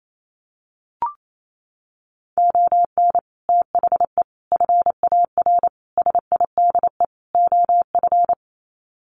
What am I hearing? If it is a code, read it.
Morse code, heard as a keyed tone, about 28 wpm, 700 Hz. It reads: ON THE FAR SIDE OF
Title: Sets of 5 Words 28wpm